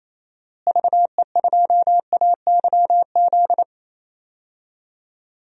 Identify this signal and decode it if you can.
Morse code, heard as a keyed tone, about 28 wpm, 700 Hz. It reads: VE2AYZ
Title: DXCC-100 Call Signs - 6 Characters - Repeated 3x - 28wpm